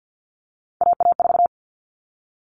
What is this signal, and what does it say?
Morse code, heard as a keyed tone, about 50 wpm, 700 Hz. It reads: AA4
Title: All Cut Numbers - 3-Digit 50wpm